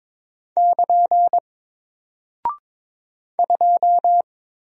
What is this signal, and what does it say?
Morse code, heard as a keyed tone, about 22 wpm, 700 Hz. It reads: <KN> 2
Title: Single Character (Letter:Number:Punctuation:Prosign) 22wpm